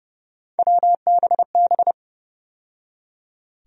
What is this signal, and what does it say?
Morse code, heard as a keyed tone, about 29 wpm, 700 Hz. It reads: WBB
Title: USA Call Sign Postfixes - 3 Letters 30wpm